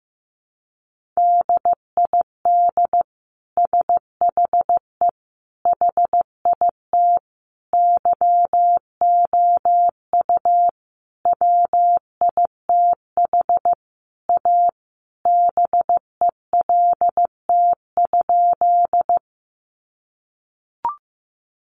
Morse code, heard as a keyed tone, about 15 wpm, 700 Hz. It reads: DID SHE HIT YOU WITH A BELT?
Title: Sentences from Top 1400 Words 15wpm